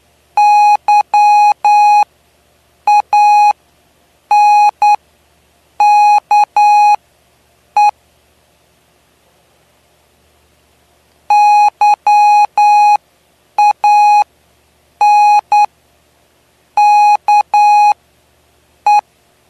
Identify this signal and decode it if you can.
Morse code, heard as a keyed tone, about 10 wpm, spaced as for 6 wpm, 830 Hz. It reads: YANKE YANKE
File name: cw / py2lcd / 09-PY2LCD-Li��o 9